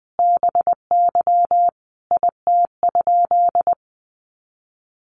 Morse code, heard as a keyed tone, about 20 wpm, 700 Hz. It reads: BY IT?
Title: Sentences from Top 100 Words 20wpm